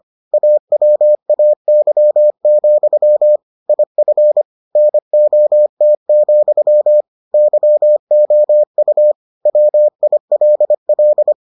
Morse code, heard as a keyed tone, about 25 wpm, 600 Hz. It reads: AWAY, IF NOT, YOU WILL